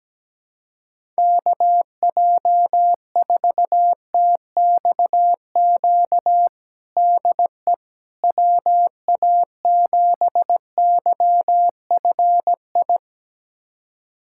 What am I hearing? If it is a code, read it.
Morse code, heard as a keyed tone, about 17 wpm, 700 Hz. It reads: KJ4TXQ DE WA7YFI
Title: USA Callsign DE Callsign - 6 Characters - 17wpm